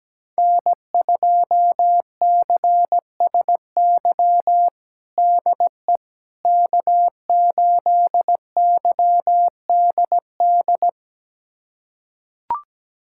Morse code, heard as a keyed tone, about 17 wpm, 700 Hz. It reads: N2CSY DE K8YDD
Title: USA Callsign DE Callsign - 5 Characters - 17wpm